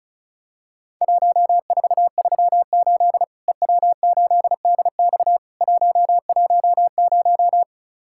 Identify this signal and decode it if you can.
Morse code, heard as a keyed tone, about 34 wpm, 700 Hz. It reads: 1438 EW8DX 110
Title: Worked All Europe DX Contest - QTC Exchange - 35wpm